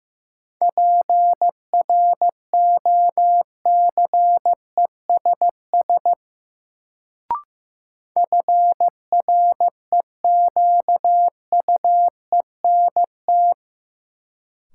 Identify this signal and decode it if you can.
Morse code, heard as a keyed tone, about 15 wpm, 700 Hz. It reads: PROCESS FREQUENT
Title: Sentences from Top 1400 Words - New Word Drill Alt - 15wpm